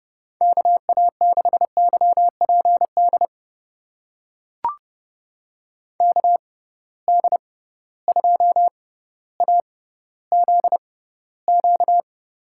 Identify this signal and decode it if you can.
Morse code, heard as a keyed tone, about 31 wpm, spaced as for 16 wpm, 700 Hz. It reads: KA6YPD KD2AZQ
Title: USA Call Signs - 6 Characters - 6x Character Spacing - 30wpm